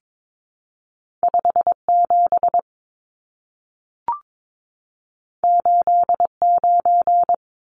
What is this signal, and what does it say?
Morse code, heard as a keyed tone, about 22 wpm, 700 Hz. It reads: 57 89
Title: Numbers - 2-Digit 22wpm